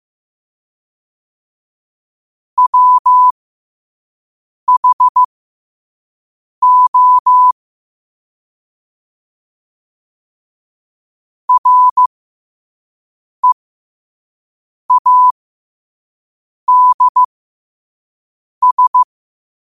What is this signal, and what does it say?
Morse code, heard as a keyed tone, about 15 wpm, spaced as for 5 wpm, 1000 Hz. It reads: WHO READS
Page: New quotes every day in morse code at 5 Words per minute.